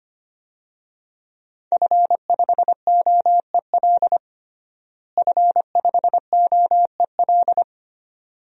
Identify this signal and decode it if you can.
Morse code, heard as a keyed tone, about 25 wpm, 700 Hz. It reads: F5OEL F5OEL
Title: DXCC-100 Call Signs - 5 Characters - Repeated 2x - 25wpm